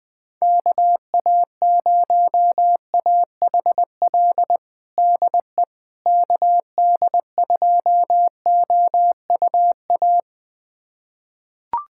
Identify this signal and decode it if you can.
Morse code, heard as a keyed tone, about 21 wpm, 700 Hz. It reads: KA0AHL DE KD2OUA
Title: USA Callsign DE Callsign 20wpm